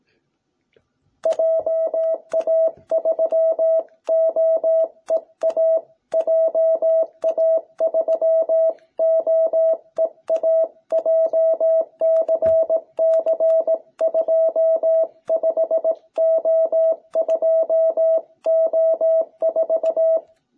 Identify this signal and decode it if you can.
Morse code, heard as a keyed tone, about 18 wpm, 660 Hz. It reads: JA3OEA JA3OEA JCC25O2O4